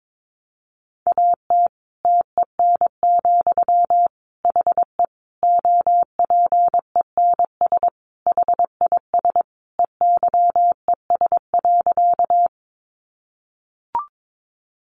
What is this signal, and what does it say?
Morse code, heard as a keyed tone, about 22 wpm, 700 Hz. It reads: AT TEN, HE OPENS HIS EYES.
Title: Easy Phrases 22wpm